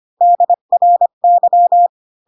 Morse code, heard as a keyed tone, about 25 wpm, 700 Hz. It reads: DRY
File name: Pinocchio / Pinocchio ch7 20 25 ik1hge